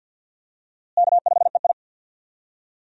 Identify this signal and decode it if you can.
Morse code, heard as a keyed tone, about 53 wpm, 700 Hz. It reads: K5EI
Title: USA Call Signs - 4 Characters - Repeated 3x - 50wpm